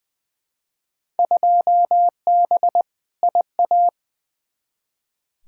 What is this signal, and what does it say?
Morse code, heard as a keyed tone, about 20 wpm, 700 Hz. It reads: 2B IA
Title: Field Day Exchanges Encore - 20wpm